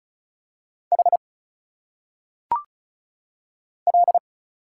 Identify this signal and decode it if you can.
Morse code, heard as a keyed tone, about 36 wpm, 700 Hz. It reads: H L
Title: Single Letters - Phonetic Pronunciation 35wpm